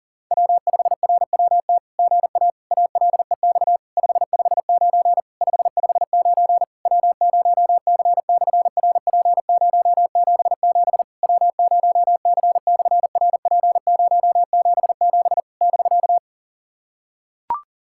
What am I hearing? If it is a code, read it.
Morse code, heard as a keyed tone, about 39 wpm, 700 Hz. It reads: W5RWT GA ALEX 559 559 W0C/RP077 W0C/RP077 <BK>
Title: SOTA QSO Practice 40wpm